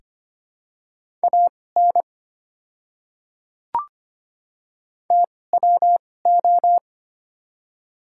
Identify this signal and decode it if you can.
Morse code, heard as a keyed tone, about 25 wpm, spaced as for 18 wpm, 700 Hz. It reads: AN TWO
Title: 2x Character Spacing - Top 100 Words - 25wpm